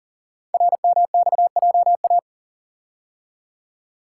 Morse code, heard as a keyed tone, about 39 wpm, 700 Hz. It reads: RMXJA
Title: Code Groups - 5 Letters - 40wpm